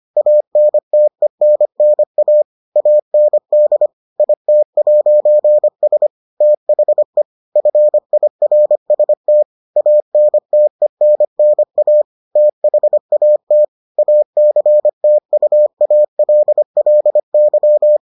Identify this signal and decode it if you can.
Morse code, heard as a keyed tone, about 25 wpm, 600 Hz. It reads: ANTENNA AND IT'S THE FIRST ANTENNA THAT ACTUALLY